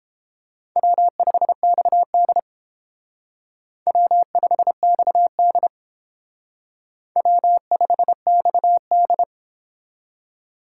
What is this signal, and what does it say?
Morse code, heard as a keyed tone, about 29 wpm, 700 Hz. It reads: W5XD W5XD W5XD